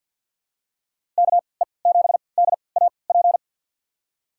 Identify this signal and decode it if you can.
Morse code, heard as a keyed tone, about 52 wpm, spaced as for 29 wpm, 700 Hz. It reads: KE7DAP
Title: USA Call Signs - 6 Characters - 3x Character Spacing - 50wpm